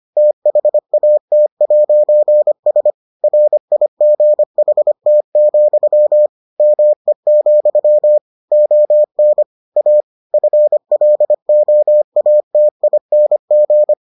Morse code, heard as a keyed tone, about 25 wpm, 600 Hz. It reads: THAT'S RIGHT, ME, ON A FLOATING